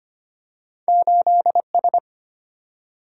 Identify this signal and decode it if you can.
Morse code, heard as a keyed tone, about 25 wpm, 700 Hz. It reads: 8S
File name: Two Letter-Number Combinations - 25wpm